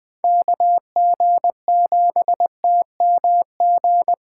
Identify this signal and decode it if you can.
Morse code, heard as a keyed tone, about 20 wpm, 700 Hz. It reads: KG7TMG
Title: USA Call Signs - 6 Characters - 6x Character Spacing - 20wpm